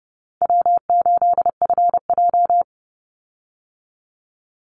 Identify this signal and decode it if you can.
Morse code, heard as a keyed tone, about 30 wpm, 700 Hz. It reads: W8FJ
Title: Easy Call Signs Repeated 2x - 30wpm